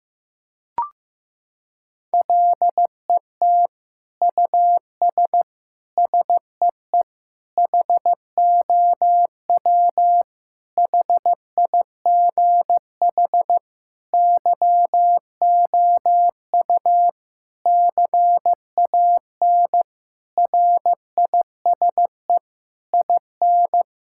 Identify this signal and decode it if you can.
Morse code, heard as a keyed tone, about 15 wpm, 700 Hz. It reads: LET US SEE HOW HIGH YOU CAN RISE IN
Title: Sentences from Top 1000 Words - Review Part 6 - 15wpm